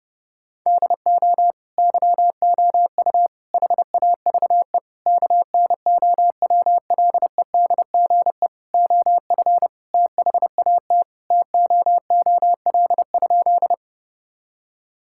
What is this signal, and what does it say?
Morse code, heard as a keyed tone, about 32 wpm, 700 Hz. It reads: DO YOU HAVE KNOWLEDGE OF THAT TOOL?
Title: Sentences from Top 1000 Words - Review Part 6 - 30wpm